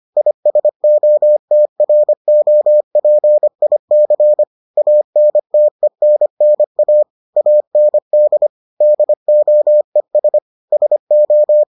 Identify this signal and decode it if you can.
Morse code, heard as a keyed tone, about 25 wpm, 600 Hz. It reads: ISOTROPIC ANTENNA AND DOES SO